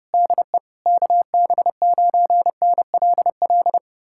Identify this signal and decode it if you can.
Morse code, heard as a keyed tone, about 35 wpm, 700 Hz. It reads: DE KB9NLL